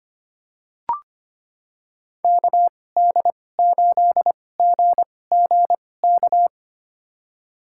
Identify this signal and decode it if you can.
Morse code, heard as a keyed tone, about 25 wpm, spaced as for 18 wpm, 700 Hz. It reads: KD8GGK
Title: USA Call Signs - 6 Characters - 2x Character Spacing - 25wpm